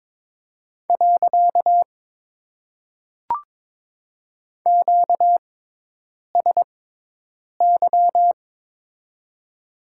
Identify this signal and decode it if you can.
Morse code, heard as a keyed tone, about 22 wpm, 700 Hz. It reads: . Q S Y
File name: Top 90 QSO Elements - 6x Character Spacing - 22wpm